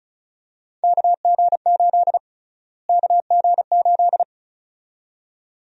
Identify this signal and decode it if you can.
Morse code, heard as a keyed tone, about 36 wpm, 700 Hz. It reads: KG8 KG8
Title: USA Call Sign Prefixes - Repeated 2x - 35wpm